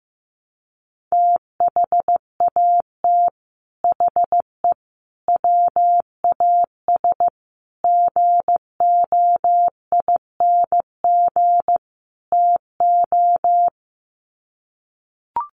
Morse code, heard as a keyed tone, about 15 wpm, 700 Hz. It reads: THAT HE WAS GOING TO
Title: Sets of 5 Words 15wpm